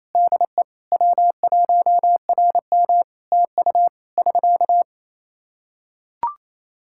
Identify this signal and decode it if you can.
Morse code, heard as a keyed tone, about 28 wpm, 700 Hz. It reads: DE W1RM TU <SK>
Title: Practice QSO - Exchange 4 - Ending - 28wpm